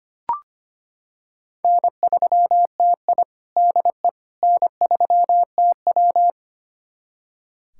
Morse code, heard as a keyed tone, about 25 wpm, 700 Hz. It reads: N3TI DE N3TW
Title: USA Callsign DE Callsign - 4 Characters - 25wpm